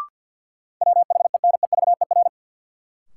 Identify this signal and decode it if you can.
Morse code, heard as a keyed tone, about 48 wpm, 700 Hz. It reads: WHENEVER
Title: Sentences from Top 1200 Words - New Word Drill - 50wpm